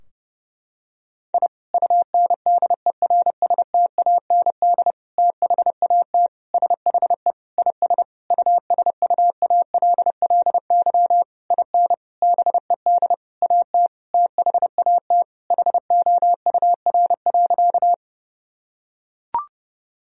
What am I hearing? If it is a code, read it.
Morse code, heard as a keyed tone, about 32 wpm, 700 Hz. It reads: I UNDERSTAND THAT SHE IS USUALLY IN BED AT THAT HOUR.
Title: Sentences from Top 1000 Words - Review Part 3 - 30wpm